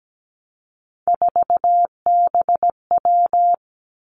Morse code, heard as a keyed tone, about 17 wpm, 700 Hz. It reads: 4BW
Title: Three Challenging Letter-Number Combinations 17wpm